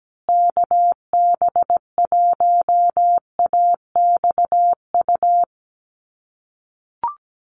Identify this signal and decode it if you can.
Morse code, heard as a keyed tone, about 17 wpm, 700 Hz. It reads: KB1AXU
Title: USA Call Signs Repeated 3x 17wpm